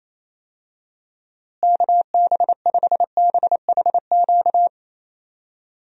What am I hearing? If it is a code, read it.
Morse code, heard as a keyed tone, about 28 wpm, 700 Hz. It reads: KB5BHQ
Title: USA Call Signs - 6 Characters - Repeated 3x - 28wpm